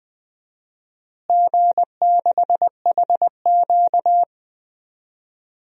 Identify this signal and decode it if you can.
Morse code, heard as a keyed tone, about 21 wpm, 700 Hz. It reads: G6HQ
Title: DXCC-100 Call Signs - 4 Characters - Repeated 2x - 20wpm